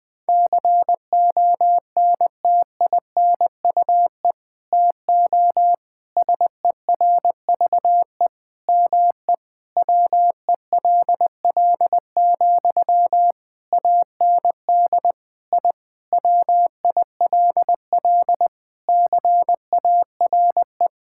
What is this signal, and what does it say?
Morse code, heard as a keyed tone, about 20 wpm, 700 Hz. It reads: CONTINUE TO SERVE ME WELL, AND I WILL CARE
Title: Sentences from Top 1000 Words - Review Part 6 - 20wpm